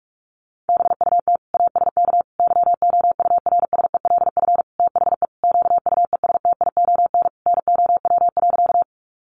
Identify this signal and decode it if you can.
Morse code, heard as a keyed tone, about 44 wpm, 700 Hz. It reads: BUT ASK YOURSELF THE QUESTION NOW.
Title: Sentences from Top 600 Words 45wpm